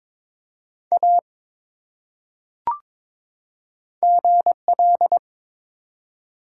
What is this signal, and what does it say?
Morse code, heard as a keyed tone, about 22 wpm, 700 Hz. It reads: A GL
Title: Top 180 QSO Elements 22wpm